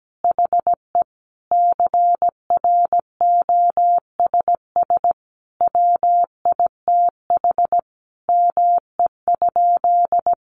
Morse code, heard as a keyed tone, about 17 wpm, 700 Hz. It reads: HE CROSS WITH ME?
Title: Sentences from Top 300 Words 17wpm